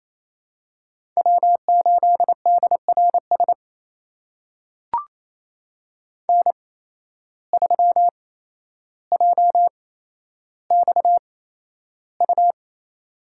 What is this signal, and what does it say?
Morse code, heard as a keyed tone, about 28 wpm, spaced as for 12 wpm, 700 Hz. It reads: W8DRS N3JXU
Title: USA Call Signs - 5 Characters - 8x Character Spacing - 28wpm